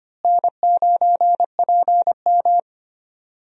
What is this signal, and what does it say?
Morse code, heard as a keyed tone, about 25 wpm, 700 Hz. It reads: N9PM
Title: USA Call Signs - 4 Characters - Repeated 2x - 25wpm